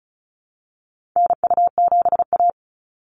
Morse code, heard as a keyed tone, about 34 wpm, 700 Hz. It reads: NU7A
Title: NA SOTA Activator Callsigns 35wpm